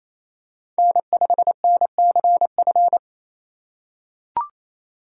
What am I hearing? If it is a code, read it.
Morse code, heard as a keyed tone, about 28 wpm, 700 Hz. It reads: N5NCF
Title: USA Call Signs - 5 Characters - 2x Character Spacing - 28wpm